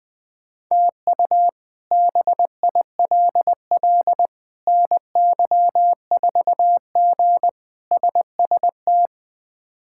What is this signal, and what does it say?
Morse code, heard as a keyed tone, about 20 wpm, 700 Hz. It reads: TU BILL NY4G SST
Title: K1USN SST QSO Practice - 20wpm